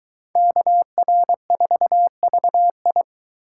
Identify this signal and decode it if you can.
Morse code, heard as a keyed tone, about 22 wpm, 700 Hz. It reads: KR4VI